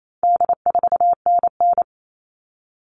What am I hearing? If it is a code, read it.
Morse code, heard as a keyed tone, about 28 wpm, 700 Hz. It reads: D4NN
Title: All Cut Numbers - 4-Digit 28wpm